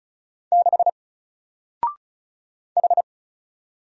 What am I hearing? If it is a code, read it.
Morse code, heard as a keyed tone, about 35 wpm, 700 Hz. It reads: 6 H
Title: Single Character (Letter:Number:Punctuation:Prosign) 35wpm